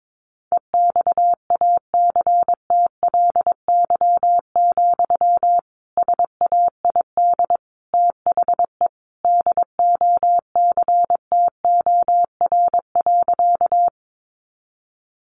Morse code, heard as a keyed tone, about 22 wpm, 700 Hz. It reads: EXACTLY, SAID THE DOCTOR.
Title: Sentences from Top 1500 Words 22wpm